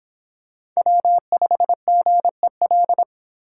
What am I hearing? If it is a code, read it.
Morse code, heard as a keyed tone, about 24 wpm, 700 Hz. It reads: W5GEL